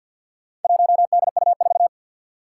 Morse code, heard as a keyed tone, about 47 wpm, 700 Hz. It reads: 1DU4
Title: Four Challenging Letter-Number Combinations 50wpm